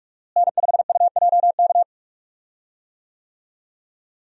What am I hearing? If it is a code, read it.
Morse code, heard as a keyed tone, about 44 wpm, 700 Hz. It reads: N5UJK
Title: USA Call Signs - 5 Characters - Repeated 3x - 45wpm